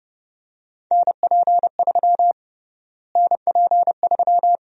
Morse code, heard as a keyed tone, about 30 wpm, 700 Hz. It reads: NP3 NP3
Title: USA Call Sign Prefixes - Repeated 2x - 30wpm